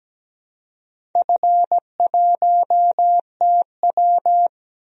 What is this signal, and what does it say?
Morse code, heard as a keyed tone, about 17 wpm, 700 Hz. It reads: F1TW
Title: DXCC-100 Call Signs - 4 Characters - 17wpm